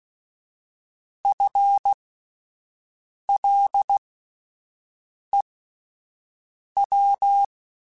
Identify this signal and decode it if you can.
Morse code, heard as a keyed tone, about 16 wpm, spaced as for 5 wpm, 800 Hz. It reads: FLEW
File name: Spaced / FL Runs sp mp3s / FL Runs sp @10